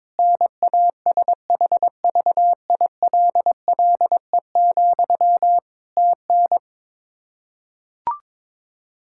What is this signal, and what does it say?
Morse code, heard as a keyed tone, about 21 wpm, 700 Hz. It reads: NASHVILLE, TN
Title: Practice QSO - Exchange 1 - Beginning - 22wpm